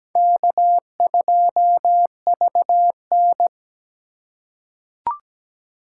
Morse code, heard as a keyed tone, about 17 wpm, 700 Hz. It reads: K2VN
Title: USA Call Signs - 4 Characters - 3x Character Spacing - 17wpm